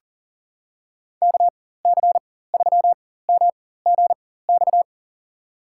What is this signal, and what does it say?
Morse code, heard as a keyed tone, about 39 wpm, spaced as for 19 wpm, 690 Hz. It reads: KC3MGX
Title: USA Call Signs - 4x Character Spacing - 40wpm